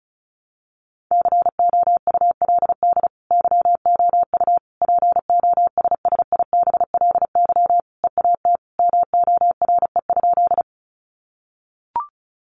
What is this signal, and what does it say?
Morse code, heard as a keyed tone, about 34 wpm, 700 Hz. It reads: COULD YOU POSSIBLY EAT MORE?
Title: Sentences from Top 1300 Words 35wpm